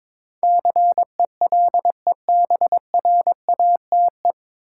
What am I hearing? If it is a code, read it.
Morse code, heard as a keyed tone, about 21 wpm, 700 Hz. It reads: CELEBRATE
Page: Morse Code Ninja practice 22wpm